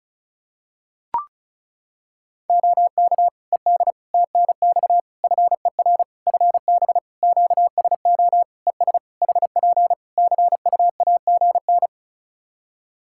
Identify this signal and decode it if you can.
Morse code, heard as a keyed tone, about 34 wpm, 700 Hz. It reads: OK ED TNX FER FB QSO ES HP CUAGN
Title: Practice QSO - Exchange 4 - Ending - 35wpm